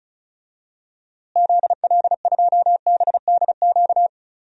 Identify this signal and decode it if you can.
Morse code, heard as a keyed tone, about 36 wpm, 700 Hz. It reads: ZL2BDQ
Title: DXCC-100 Call Signs - 6 Characters - Repeated 2x - 35wpm